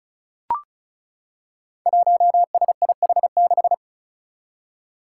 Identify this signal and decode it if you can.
Morse code, heard as a keyed tone, about 36 wpm, 700 Hz. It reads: 1SIH6
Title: Five Challenging Letter-Number Combinations 35wpm